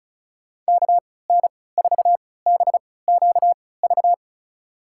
Morse code, heard as a keyed tone, about 34 wpm, spaced as for 20 wpm, 700 Hz. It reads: KN4BQV
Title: USA Call Signs - 3x Character Spacing - 35wpm